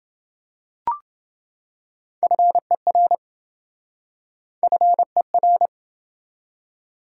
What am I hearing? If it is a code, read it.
Morse code, heard as a keyed tone, about 27 wpm, 700 Hz. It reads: FER FER